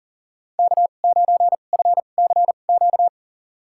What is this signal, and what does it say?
Morse code, heard as a keyed tone, about 38 wpm, spaced as for 28 wpm, 700 Hz. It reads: K9FCQ
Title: USA Call Signs - 5 Characters - 2x Character Spacing - 40wpm